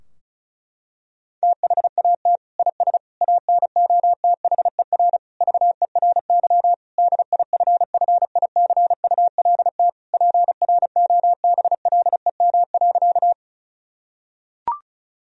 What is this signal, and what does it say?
Morse code, heard as a keyed tone, about 34 wpm, 700 Hz. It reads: THAT IS ANOTHER VERY DIFFICULT PROBLEM.
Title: Sentences from Top 1000 Words - Review Part 2 - 35wpm